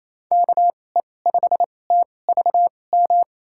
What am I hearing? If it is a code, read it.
Morse code, heard as a keyed tone, about 28 wpm, spaced as for 20 wpm, 700 Hz. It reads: KE5TVM
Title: USA Call Signs - 6 Characters - 2x Character Spacing - 28wpm